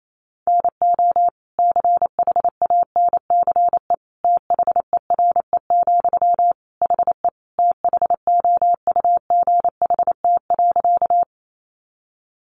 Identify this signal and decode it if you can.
Morse code, heard as a keyed tone, about 28 wpm, 700 Hz. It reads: NO CHANCE THERE, HE THOUGHT.
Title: Sentences from Top 800 Words 28wpm